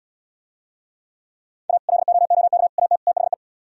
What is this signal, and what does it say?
Morse code, heard as a keyed tone, about 77 wpm, 700 Hz. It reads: A LOOK AT THE